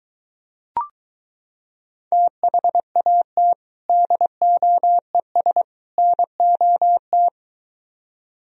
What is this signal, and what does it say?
Morse code, heard as a keyed tone, about 22 wpm, 700 Hz. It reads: THAT DOES NOT